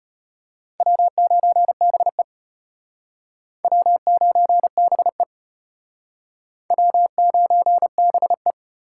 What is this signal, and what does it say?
Morse code, heard as a keyed tone, about 33 wpm, 700 Hz. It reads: W9BE W9BE W9BE